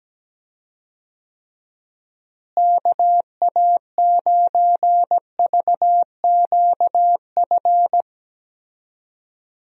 Morse code, heard as a keyed tone, about 17 wpm, 700 Hz. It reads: KA9VQF
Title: USA Call Signs - 6 Characters - Repeated 2x - 17wpm